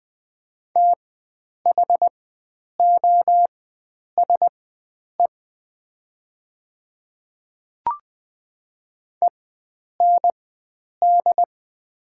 Morse code, heard as a keyed tone, about 20 wpm, spaced as for 9 wpm, 700 Hz. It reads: THOSE END
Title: 4x Character Spacing - Top 300 Words - 20wpm